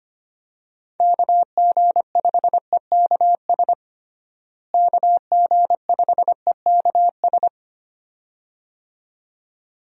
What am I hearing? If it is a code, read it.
Morse code, heard as a keyed tone, about 25 wpm, 700 Hz. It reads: KG5EKS KG5EKS
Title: USA Call Signs - 6 Characters - Repeated 2x - 25wpm